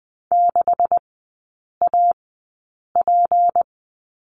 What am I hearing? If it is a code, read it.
Morse code, heard as a keyed tone, about 20 wpm, spaced as for 8 wpm, 700 Hz. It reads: 6AP
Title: 152 Lesson 25e - Letter p - Three Characters 20wpm